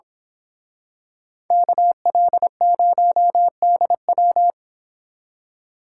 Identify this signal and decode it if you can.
Morse code, heard as a keyed tone, about 26 wpm, 690 Hz. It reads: KL0DW
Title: Speed Racing - USA Call Signs 22wpm